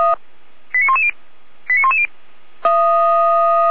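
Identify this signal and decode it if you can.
Five-tone selective call (ZVEI), heard as quick runs of five tones, tones of about 70 ms. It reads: 89110 89110
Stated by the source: Received on an SDR as a ZVEI five-tone call.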